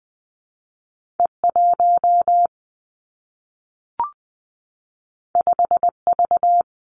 Morse code, heard as a keyed tone, about 19 wpm, 700 Hz. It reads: E 1 5 V
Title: Two Challenging Letter-Number Combinations 20wpm